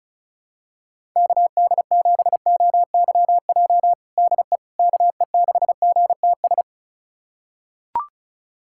Morse code, heard as a keyed tone, about 34 wpm, 700 Hz. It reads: KD7OYJ DE KE6GTS
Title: USA Callsign DE Callsign - 6 Characters - 35wpm